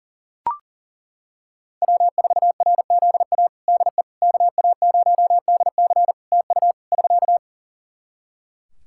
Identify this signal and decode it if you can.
Morse code, heard as a keyed tone, about 40 wpm, 700 Hz. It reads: W4RZA DE KA0DC TU <SK>
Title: Practice QSO - Exchange 4 - Ending - 40wpm